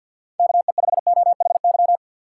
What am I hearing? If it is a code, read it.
Morse code, heard as a keyed tone, about 53 wpm, 700 Hz. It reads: KE5OHY
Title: USA Call Signs - 6 Characters - 4x Character Spacing - 50wpm